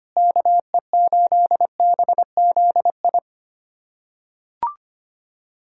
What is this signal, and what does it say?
Morse code, heard as a keyed tone, about 25 wpm, 700 Hz. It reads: KE8BZI
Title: USA Callsign DE Callsign 25wpm